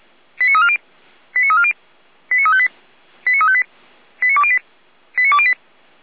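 Five-tone selective call (ZVEI), heard as quick runs of five tones, tones of about 70 ms. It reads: 89340 89340 89358 89358 89208 89208
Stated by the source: ZVEI five-tone selective call received on an SDR.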